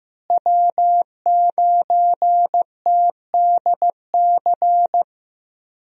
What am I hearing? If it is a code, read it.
Morse code, heard as a keyed tone, about 15 wpm, 700 Hz. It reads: W9TDC
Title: USA Call Signs - 5 Characters - Repeated 2x - 15wpm